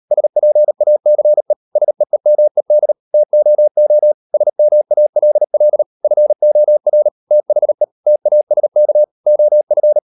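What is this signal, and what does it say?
Morse code, heard as a keyed tone, about 38 wpm, 600 Hz. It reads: SPACE SEEMED TOO SMALL FOR THE TASK OF
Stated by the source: Morse code practice at 38 WPM, sending Ken Follett's novel Never in English, from CW med Gnister.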